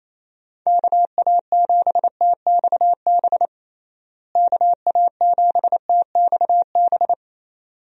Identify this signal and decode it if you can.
Morse code, heard as a keyed tone, about 28 wpm, 700 Hz. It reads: KA7TXB KA7TXB
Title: USA Call Signs - 6 Characters - Repeated 2x - 28wpm